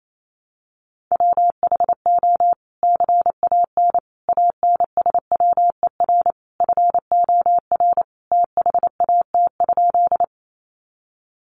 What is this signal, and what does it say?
Morse code, heard as a keyed tone, about 28 wpm, 700 Hz. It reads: WHO CAN ANSWER FOR THAT?
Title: Sentences from Top 300 Words 28wpm